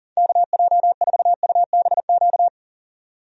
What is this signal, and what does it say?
Morse code, heard as a keyed tone, about 40 wpm, 700 Hz. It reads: KJ4UBQ
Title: USA Call Signs - 6 Characters - 40wpm